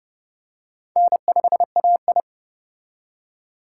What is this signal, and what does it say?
Morse code, heard as a keyed tone, about 29 wpm, 690 Hz. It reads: N5AI